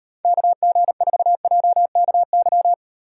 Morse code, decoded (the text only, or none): KG4JKY